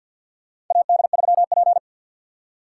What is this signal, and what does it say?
Morse code, heard as a keyed tone, about 52 wpm, 700 Hz. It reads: AD3P
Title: USA Call Signs - 4 Characters - 6x Character Spacing - 50wpm